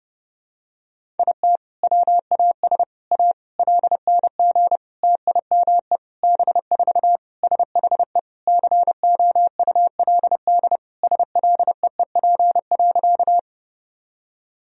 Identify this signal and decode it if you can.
Morse code, heard as a keyed tone, about 30 wpm, 700 Hz. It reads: IT WAS A LNG TIME B4 SHE COULD SLEEP.
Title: Sentences from Top 500 Words and Abbreviations 30wpm